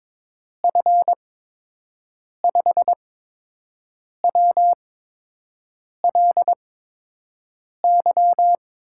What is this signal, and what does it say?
Morse code, heard as a keyed tone, about 22 wpm, spaced as for 6 wpm, 700 Hz. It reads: F5WLY
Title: USA Call Signs - 6 Characters - 8x Character Spacing - 22wpm